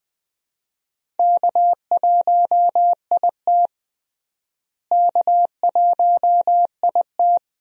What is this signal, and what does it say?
Morse code, heard as a keyed tone, about 21 wpm, 700 Hz. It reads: K1IT K1IT
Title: USA Call Signs - 4 Characters - Repeated 2x - 20wpm